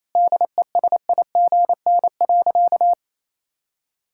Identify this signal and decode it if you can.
Morse code, heard as a keyed tone, about 28 wpm, 700 Hz. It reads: DESIGN.
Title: Sentences from Top 1000 Words - Review Part 5 - 28wpm